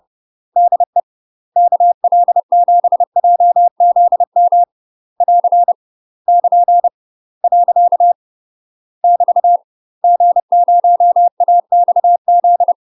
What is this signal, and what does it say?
Morse code, heard as a keyed tone, about 30 wpm, 700 Hz. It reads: DE KL7JZM <AR> <KN> . <BT> G0AXZ
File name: UK MASTER QSOs TRAINING/30WPM